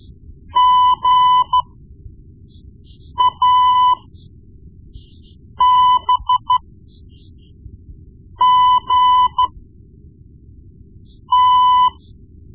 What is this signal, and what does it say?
Morse code, heard as a keyed tone, about 13 wpm, spaced as for 4 wpm, 1000 Hz. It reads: GABGT
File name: Morse Lessons / Morse Lessons / Tape-1B